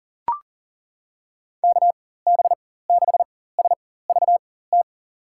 Morse code, heard as a keyed tone, about 40 wpm, spaced as for 19 wpm, 700 Hz. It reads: KB6SVT
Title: USA Call Signs - 6 Characters - 4x Character Spacing - 40wpm